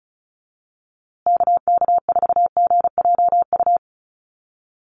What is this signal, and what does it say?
Morse code, heard as a keyed tone, about 34 wpm, 700 Hz. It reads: KK4GJU
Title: USA Call Signs Repeated 3x 35wpm